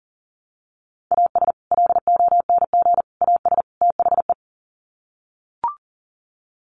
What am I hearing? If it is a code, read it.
Morse code, heard as a keyed tone, about 40 wpm, 700 Hz. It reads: AS LONG AS THE
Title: Sets of 4 Words 40wpm